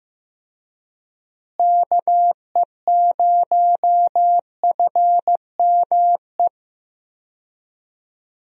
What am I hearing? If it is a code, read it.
Morse code, heard as a keyed tone, about 15 wpm, 700 Hz. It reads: KE0FME
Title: USA Call Signs - 6 Characters - Repeated 3x - 15wpm